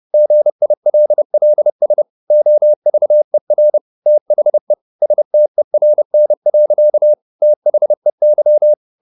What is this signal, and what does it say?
Morse code, heard as a keyed tone, about 30 wpm, 600 Hz. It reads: GILLS OVER THE STERN. THEY